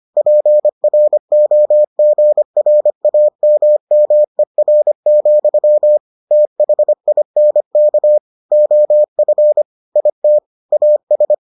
Morse code, heard as a keyed tone, about 25 wpm, 600 Hz. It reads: PROGRAMMER, THINK OF IT AS